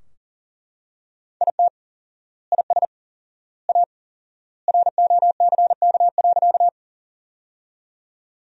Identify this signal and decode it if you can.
Morse code, heard as a keyed tone, about 40 wpm, 700 Hz. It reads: IT IS A ROCK.
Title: 4x Word-Spacing - Sentences from Top 400 Words - 40wpm